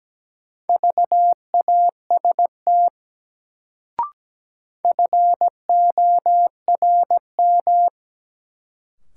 Morse code, heard as a keyed tone, about 17 wpm, 700 Hz. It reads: VAST FORM
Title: Sets of 4 Words - New Word Drill - 17wpm